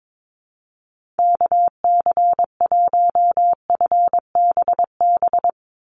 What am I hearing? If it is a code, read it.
Morse code, heard as a keyed tone, about 22 wpm, 700 Hz. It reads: KC1FBB
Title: USA Call Signs 22wpm